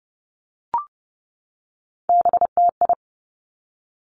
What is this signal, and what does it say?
Morse code, heard as a keyed tone, about 30 wpm, 700 Hz. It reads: BTI
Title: Three Challenging Letter-Number Combinations 30wpm